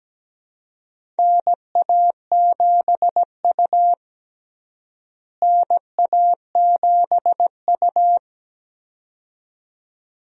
Morse code, heard as a keyed tone, about 17 wpm, 700 Hz. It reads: NA7U NA7U
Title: USA Call Signs - 4 Characters - Repeated 2x - 17wpm